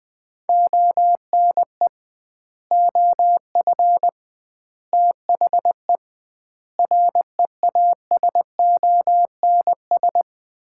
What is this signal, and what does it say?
Morse code, heard as a keyed tone, about 20 wpm, 700 Hz. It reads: ONE OF THE REASONS
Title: 2x Word-Spacing - Sets of 5 Words - 20wpm